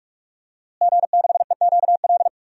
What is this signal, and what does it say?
Morse code, heard as a keyed tone, about 46 wpm, 700 Hz. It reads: G6EQL
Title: DXCC-100 Call Signs - 5 Characters - Repeated 3x - 45wpm